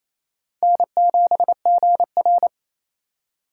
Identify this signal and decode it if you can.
Morse code, heard as a keyed tone, about 35 wpm, 700 Hz. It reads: N7GR
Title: Speed Racing - USA 1x2 Call Signs 28wpm